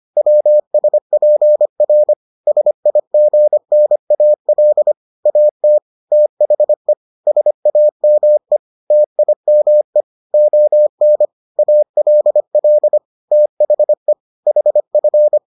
Morse code, heard as a keyed tone, about 25 wpm, 600 Hz. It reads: WSPR SIGNAL AT THE SAME TIME ON ALL THE HF